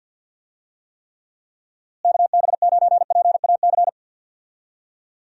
Morse code, heard as a keyed tone, about 49 wpm, 700 Hz. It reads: KB9PAC
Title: USA Call Signs - 6 Characters - Repeated 2x - 50wpm